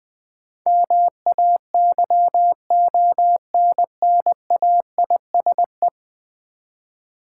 Morse code, heard as a keyed tone, about 20 wpm, 700 Hz. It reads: MAYONNAISE
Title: Easy Words with Hints - 20wpm